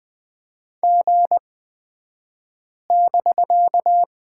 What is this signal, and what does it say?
Morse code, heard as a keyed tone, about 20 wpm, spaced as for 5 wpm, 700 Hz. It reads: G<BK>
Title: 249 Lesson 40d - BK - Two Characters 20wpm